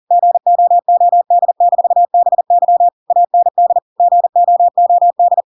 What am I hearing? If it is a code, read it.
Morse code, heard as a keyed tone, about 40 wpm, 700 Hz. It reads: GOOD-BY AND GOOD
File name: Pinocchio / Pinocchio ch5 40 40 ik1hge